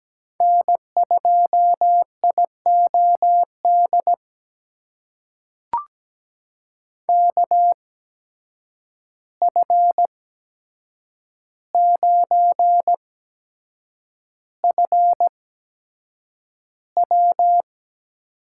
Morse code, heard as a keyed tone, about 17 wpm, spaced as for 7 wpm, 700 Hz. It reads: N2IOD KF9FW
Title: USA Call Signs - 5 Characters - 8x Character Spacing - 17wpm